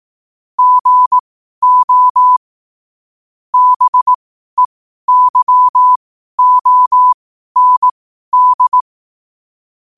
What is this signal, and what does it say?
Morse code, heard as a keyed tone, about 18 wpm, spaced as for 12 wpm, 1000 Hz. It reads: GO BEYOND
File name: qotdfiles / 12wpm